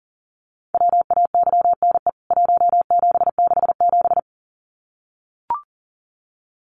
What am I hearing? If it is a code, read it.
Morse code, heard as a keyed tone, about 40 wpm, 700 Hz. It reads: WAYNE 1767
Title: CWT Exchange 40wpm